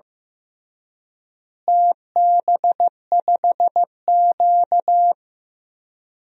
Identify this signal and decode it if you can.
Morse code, heard as a keyed tone, about 15 wpm, 700 Hz. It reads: TB5Q
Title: DXCC-100 Call Signs - 4 Characters - Repeated 3x - 15wpm